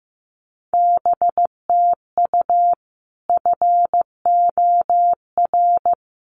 Morse code, heard as a keyed tone, about 15 wpm, 700 Hz. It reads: BTU FOR
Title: Common QSO Phrases 15wpm